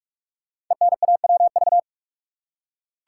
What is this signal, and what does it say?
Morse code, heard as a keyed tone, about 44 wpm, 700 Hz. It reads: ENAWV
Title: Code Groups - 5 Letters - 45wpm